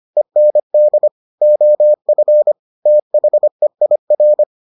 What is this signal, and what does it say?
Morse code, heard as a keyed tone, about 25 wpm, 600 Hz. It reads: END OF THEIR